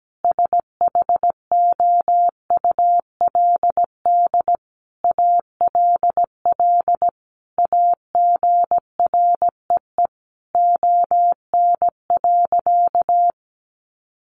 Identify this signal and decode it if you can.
Morse code, heard as a keyed tone, about 17 wpm, 700 Hz. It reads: SHOULD ALL AGREE ON.
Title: Sentences from Top 1100 Words 17wpm